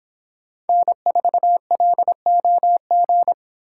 Morse code, heard as a keyed tone, about 24 wpm, 700 Hz. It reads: N4LOG